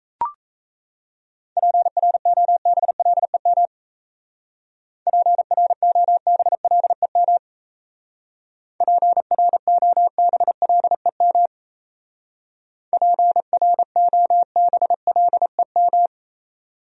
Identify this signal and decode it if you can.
Morse code, heard as a keyed tone, about 34 wpm, 700 Hz. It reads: PROBLEM PROBLEM PROBLEM PROBLEM
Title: Speed-Racing Top 500 Words 28wpm